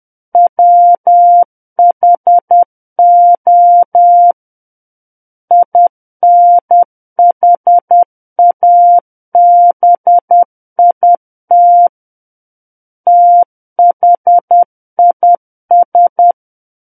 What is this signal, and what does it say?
Morse code, heard as a keyed tone, about 10 wpm, 700 Hz. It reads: WHO INHABIT THIS
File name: Book One / 01-Chapter 1 - 10 WPM